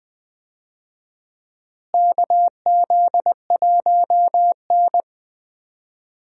Morse code, heard as a keyed tone, about 20 wpm, 700 Hz. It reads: KZ1N
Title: USA Call Signs - 4 Characters - Repeated 2x - 20wpm